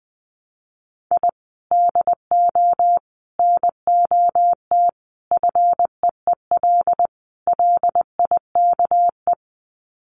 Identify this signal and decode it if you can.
Morse code, heard as a keyed tone, about 20 wpm, 700 Hz. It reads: I DO NOT FEEL LIKE
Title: Sets of 5 Words Encore 20wpm